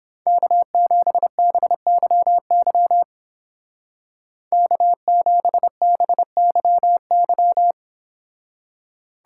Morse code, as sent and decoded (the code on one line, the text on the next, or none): -.- --... -... -.-- -.-- / -.- --... -... -.-- -.--
K7BYY K7BYY